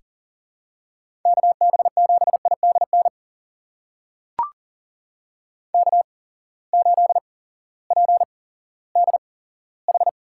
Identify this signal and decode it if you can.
Morse code, heard as a keyed tone, about 40 wpm, spaced as for 18 wpm, 700 Hz. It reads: KB7IDN K8PDH
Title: USA Call Signs - 8x Character Spacing - 40wpm